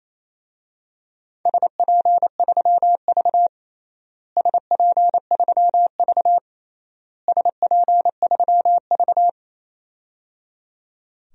Morse code, heard as a keyed tone, about 28 wpm, 700 Hz. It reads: SP3V SP3V SP3V
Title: DXCC-100 Call Signs - 4 Characters - Repeated 3x - 28wpm